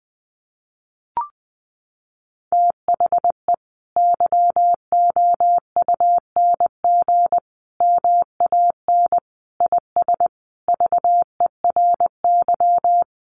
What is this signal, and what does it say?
Morse code, heard as a keyed tone, about 20 wpm, 700 Hz. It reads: THE YOUNG MAN IS VERY
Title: Sentences from Top 1200 Words 20wpm